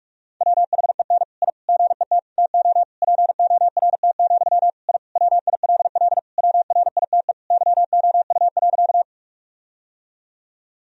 Morse code, heard as a keyed tone, about 46 wpm, 700 Hz. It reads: WHEN I GET TO PORT, I WILL WRITE YOU.
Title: Sentences from Top 1000 Words - Review Part 1 - 45wpm